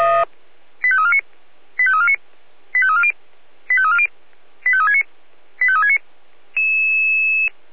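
Five-tone selective call (ZVEI), heard as quick runs of five tones, tones of about 70 ms. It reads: 85349 85349 85340 85340 85369 85369
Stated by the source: Five-tone ZVEI selective call 85360, received by SDR.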